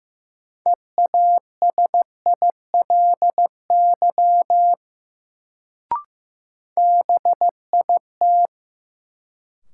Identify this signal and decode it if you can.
Morse code, heard as a keyed tone, about 16 wpm, 700 Hz. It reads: EASILY BIT
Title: Sentences from Top 900 Words - New Word Drill - 15wpm